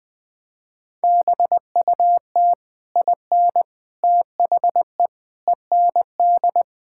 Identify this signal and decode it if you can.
Morse code, heard as a keyed tone, about 20 wpm, 700 Hz. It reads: BUT IN THE END
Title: Sentences with 3-Word Phrases - 20wpm